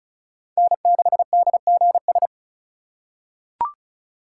Morse code, heard as a keyed tone, about 36 wpm, 700 Hz. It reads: N6DGS
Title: USA Call Signs - 5 Characters - 2x Character Spacing - 35wpm